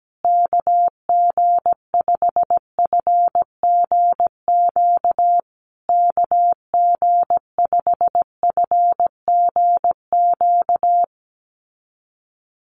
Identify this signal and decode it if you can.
Morse code, heard as a keyed tone, about 17 wpm, 700 Hz. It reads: KG5FGQ KG5FGQ
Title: USA Call Signs Repeated 3x 17wpm